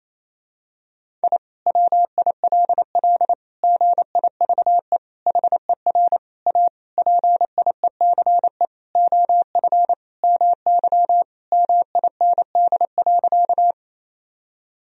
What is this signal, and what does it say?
Morse code, heard as a keyed tone, about 28 wpm, 700 Hz. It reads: I WILL GIVE HER A PIECE OF MY MIND.
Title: Sentences from Top 1000 Words - Review Part 2 - 28wpm